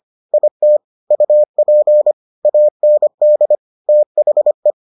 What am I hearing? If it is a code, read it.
Morse code, heard as a keyed tone, about 25 wpm, 600 Hz. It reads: IT UP AND THE